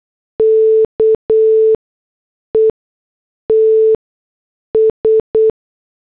Morse code, heard as a keyed tone, about 8 wpm, spaced as for 6 wpm, 440 Hz. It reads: KETS